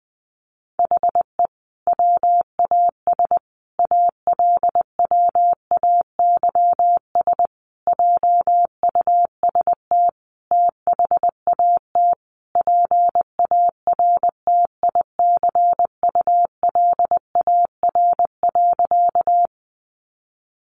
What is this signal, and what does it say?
Morse code, heard as a keyed tone, about 19 wpm, 700 Hz. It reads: HE WAS ALWAYS JUST THAT PARTICULAR.
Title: Sentences from Top 600 Words 20wpm